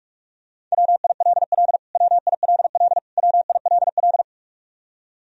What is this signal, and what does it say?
Morse code, heard as a keyed tone, about 42 wpm, 700 Hz. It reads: WILL WILL WILL
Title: Top 100 Words Repeated 45wpm